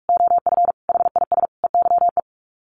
Morse code, heard as a keyed tone, about 44 wpm, 700 Hz. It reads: OF HIS EYE
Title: Sets of 5 Words Encore2 45wpm